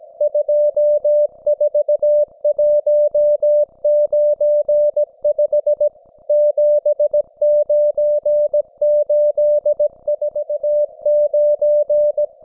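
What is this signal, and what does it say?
Morse code, heard as a keyed tone, about 17 wpm, 590 Hz. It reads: 24195 79849
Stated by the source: Russian military Morse station, M32, RDL, on 5268 kHz.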